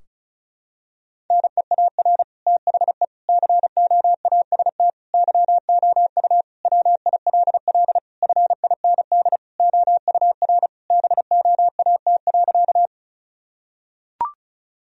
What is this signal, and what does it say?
Morse code, heard as a keyed tone, about 36 wpm, 700 Hz. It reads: NEAR THE COAST YOU WILL FIND OUR BOAT.
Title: Sentences from Top 1000 Words - Review Part 4 - 35wpm